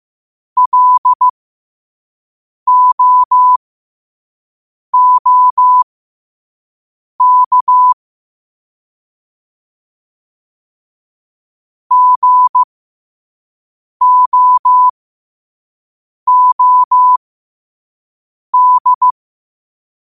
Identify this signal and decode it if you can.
Morse code, heard as a keyed tone, about 15 wpm, spaced as for 5 wpm, 1000 Hz. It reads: LOOK GOOD